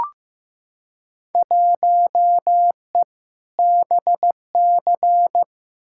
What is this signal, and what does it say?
Morse code, heard as a keyed tone, about 15 wpm, 700 Hz. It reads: 1E BC
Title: Field Day Exchanges - 15wpm